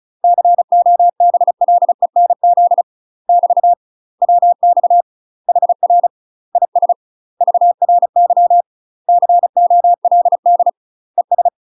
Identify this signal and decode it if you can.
Morse code, heard as a keyed tone, about 35 wpm, 700 Hz. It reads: COBLENZ <BT> WX HR IS VRY COLD ES
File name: UK MASTER QSOs TRAINING / 35WPM / UK MASTER QSOs 0032